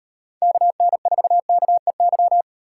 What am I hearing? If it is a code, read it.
Morse code, heard as a keyed tone, about 38 wpm, 700 Hz. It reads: KN4KEY